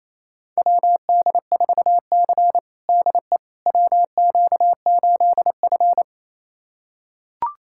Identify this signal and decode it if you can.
Morse code, heard as a keyed tone, about 28 wpm, 700 Hz. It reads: WD4C DE WQ8F
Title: USA Callsign DE Callsign - 4 Characters - 28wpm